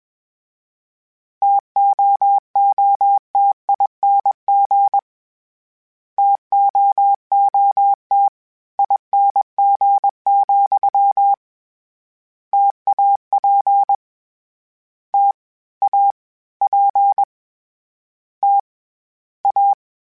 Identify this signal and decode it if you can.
Morse code, heard as a keyed tone, about 21 wpm, 800 Hz. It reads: TOOTING TOOT ING, TAP T A P T A
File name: Variable Spacing / Variable Spacing-mp3s / T AOEs @20